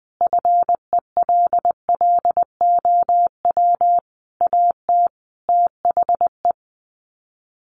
Morse code, heard as a keyed tone, about 20 wpm, 700 Hz. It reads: FELLOW AT THE
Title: Sets of 5 Words Encore 20wpm